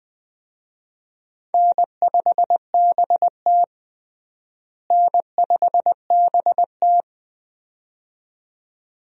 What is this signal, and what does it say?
Morse code, heard as a keyed tone, about 20 wpm, 700 Hz. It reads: N5BT N5BT
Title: USA Call Signs - 4 Characters - Repeated 2x - 20wpm